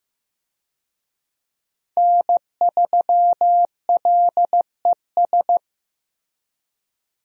Morse code, heard as a keyed tone, about 15 wpm, 700 Hz. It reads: N3LES
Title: USA Call Signs - 5 Characters - Repeated 2x - 15wpm